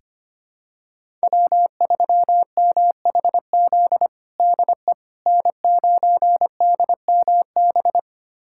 Morse code, heard as a keyed tone, about 25 wpm, 700 Hz. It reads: W3MHZ DE N9DMB
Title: USA Callsign DE Callsign - 5 Characters - 25wpm